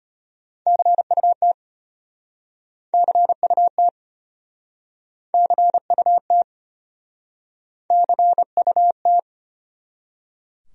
Morse code, heard as a keyed tone, about 33 wpm, 700 Hz. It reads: CUT CUT CUT CUT